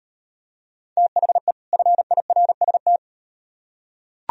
Morse code, heard as a keyed tone, about 38 wpm, 700 Hz. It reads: THE FIRST